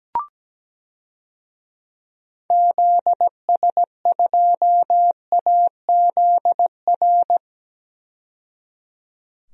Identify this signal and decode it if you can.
Morse code, heard as a keyed tone, about 17 wpm, 700 Hz. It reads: ZS2AZR
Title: DXCC-100 Call Signs - 6 Characters - 17wpm